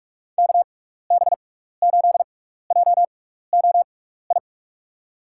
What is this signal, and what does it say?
Morse code, heard as a keyed tone, about 46 wpm, spaced as for 16 wpm, 700 Hz. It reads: KB8JOI
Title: USA Call Signs - 6 Characters - 6x Character Spacing - 45wpm